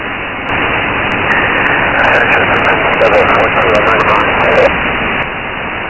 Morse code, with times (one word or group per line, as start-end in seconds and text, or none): none